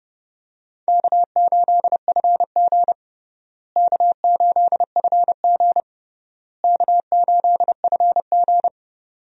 Morse code, heard as a keyed tone, about 30 wpm, 700 Hz. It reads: K8FG K8FG K8FG
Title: USA Call Signs - 4 Characters - Repeated 3x - 30wpm